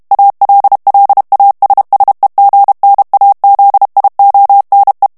Morse code, heard as a keyed tone, about 32 wpm, 800 Hz. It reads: ALLASSEGNAZIONE
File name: ITALIEN texte clair 2 vitesse 30 mots minute